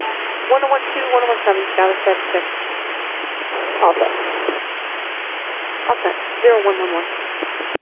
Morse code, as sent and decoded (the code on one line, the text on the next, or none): none